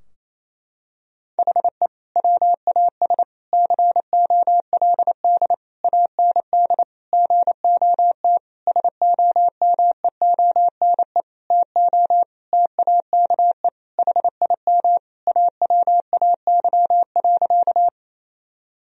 Morse code, heard as a keyed tone, about 28 wpm, 700 Hz. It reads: HE WAS COLD AND GOT SOMEONE TO TAKE HIM AWAY.
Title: Sentences from Top 1000 Words - Review Part 4 - 28wpm